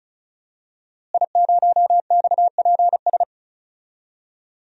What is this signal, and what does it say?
Morse code, heard as a keyed tone, about 34 wpm, 700 Hz. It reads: I0XPS
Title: DXCC-100 Call Signs - 5 Characters - Repeated 2x - 35wpm